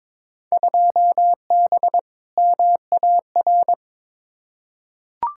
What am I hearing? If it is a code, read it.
Morse code, heard as a keyed tone, about 22 wpm, 700 Hz. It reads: 2B MAR
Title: Field Day Exchanges Encore - 22wpm